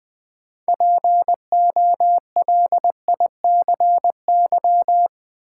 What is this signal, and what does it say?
Morse code, heard as a keyed tone, about 21 wpm, 700 Hz. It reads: POLICY
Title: Sentences from Top 1000 Words - Review Part 5 - New Word Drill - 20wpm